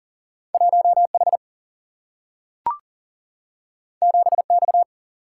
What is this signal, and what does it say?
Morse code, heard as a keyed tone, about 40 wpm, 700 Hz. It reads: 1H 7X
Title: Single-Letter Single-Number 40wpm